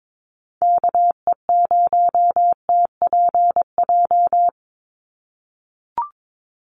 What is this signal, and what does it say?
Morse code, heard as a keyed tone, about 22 wpm, 700 Hz. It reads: KE0TPJ
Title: USA Call Signs 22wpm